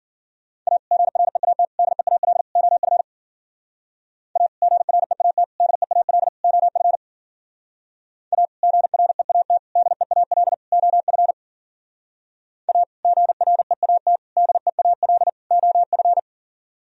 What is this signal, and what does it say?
Morse code, heard as a keyed tone, about 53 wpm, 700 Hz. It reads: A GREAT DEAL OF A GREAT DEAL OF A GREAT DEAL OF A GREAT DEAL OF